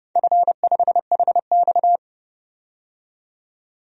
Morse code, heard as a keyed tone, about 30 wpm, 700 Hz. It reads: F5HX
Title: DXCC-100 Call Signs - 4 Characters - 30wpm